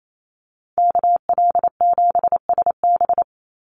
Morse code, heard as a keyed tone, about 28 wpm, 700 Hz. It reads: KL7SB
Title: Call Signs - 28wpm